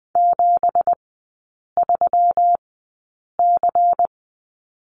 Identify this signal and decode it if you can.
Morse code, heard as a keyed tone, about 20 wpm, spaced as for 8 wpm, 690 Hz. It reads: 73C
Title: 191 Lesson 31e - Letter v - Three Characters 20wpm